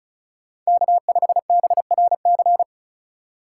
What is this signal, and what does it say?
Morse code, heard as a keyed tone, about 47 wpm, spaced as for 41 wpm, 700 Hz. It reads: K5BRC
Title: Speed Racing - USA 1x3 Call Signs 35wpm